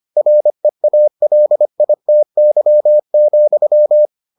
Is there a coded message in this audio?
Yes: Morse code from 0.2 to 4.1 s, heard as a keyed tone, about 25 wpm, 600 Hz. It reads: REALITY,